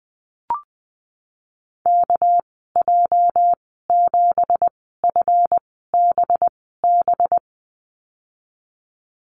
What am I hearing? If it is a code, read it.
Morse code, heard as a keyed tone, about 19 wpm, spaced as for 14 wpm, 700 Hz. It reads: KJ7FBB
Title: 231 Lesson 37g - Letter x - Call Signs 20wpm